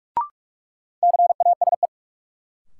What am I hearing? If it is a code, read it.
Morse code, heard as a keyed tone, about 45 wpm, 700 Hz. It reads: CASE
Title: Sentences from Top 700 Words - New Word Drill - 45wpm